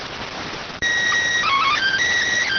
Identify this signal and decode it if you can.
SSTV (slow-scan television), PD 290.